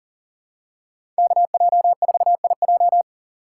Morse code, heard as a keyed tone, about 40 wpm, 700 Hz. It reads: KJ4IJ
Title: USA Call Signs - 5 Characters - 3x Character Spacing - 40wpm